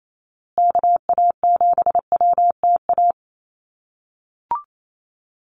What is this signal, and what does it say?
Morse code, heard as a keyed tone, about 28 wpm, 690 Hz. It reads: KA7WTA
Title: USA Call Signs Repeated 3x 28wpm